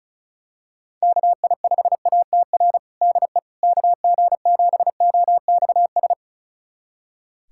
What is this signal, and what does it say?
Morse code, heard as a keyed tone, about 36 wpm, 700 Hz. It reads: KI5ATR DE KG7OXS
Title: USA Callsign DE Callsign 35wpm